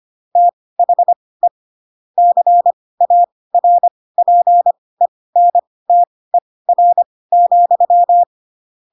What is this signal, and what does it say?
Morse code, heard as a keyed tone, about 25 wpm, spaced as for 18 wpm, 700 Hz. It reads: THE CARPENTER,
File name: Pinocchio / Pinocchio ch2 12 25 ik1hge